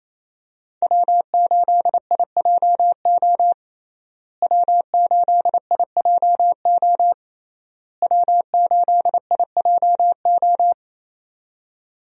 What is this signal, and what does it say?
Morse code, heard as a keyed tone, about 28 wpm, 700 Hz. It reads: W8IJO W8IJO W8IJO
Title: USA Call Signs - 5 Characters - Repeated 3x - 28wpm